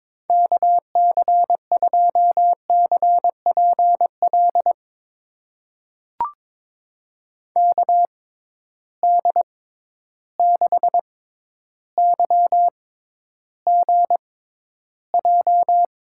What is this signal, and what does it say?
Morse code, heard as a keyed tone, about 22 wpm, spaced as for 11 wpm, 700 Hz. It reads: KC2CPL KD6YGJ
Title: USA Call Signs - 6 Characters - 6x Character Spacing - 22wpm